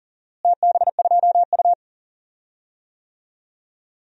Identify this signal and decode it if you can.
Morse code, heard as a keyed tone, about 39 wpm, 700 Hz. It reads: TB2U
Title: DXCC-100 Call Signs - 4 Characters - Repeated 3x - 40wpm